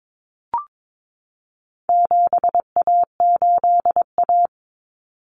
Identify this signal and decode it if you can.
Morse code, heard as a keyed tone, about 22 wpm, 700 Hz. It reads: 7A8A
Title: Common Cut Numbers - 4-Digit 22wpm